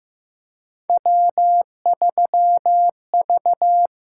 Morse code, heard as a keyed tone, about 15 wpm, 700 Hz. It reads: W3V
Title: USA Call Signs - 6x Character Spacing - 15wpm